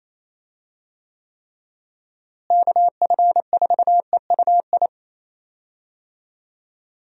Morse code, heard as a keyed tone, about 28 wpm, 700 Hz. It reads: KF4EUI
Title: USA Call Signs - 6 Characters - Repeated 2x - 28wpm